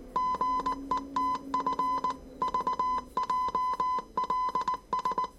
Morse code, heard as a keyed tone, about 19 wpm, 1010 Hz. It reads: GETF VJLS
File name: 业余无线电 / ●未分类 / 字码80